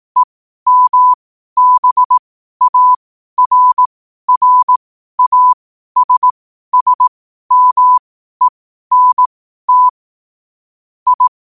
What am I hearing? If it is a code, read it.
Morse code, heard as a keyed tone, about 18 wpm, spaced as for 12 wpm, 1000 Hz. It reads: EMBARRASSMENT I
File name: qotdfiles / 12wpm